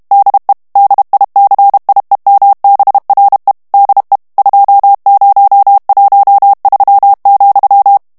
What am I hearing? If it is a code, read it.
Morse code, heard as a keyed tone, about 32 wpm, 800 Hz. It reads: DE DICIEMBRE DE 2013,
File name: morse mp3 / ESPAGNOL TEXTE EN CLAIR 1 VITESSE 30 MOTS MINUTES